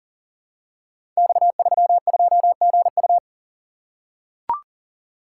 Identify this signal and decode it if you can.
Morse code, heard as a keyed tone, about 42 wpm, 700 Hz. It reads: X32GU
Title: Code Groups - 5 Letter:Numbers - 40wpm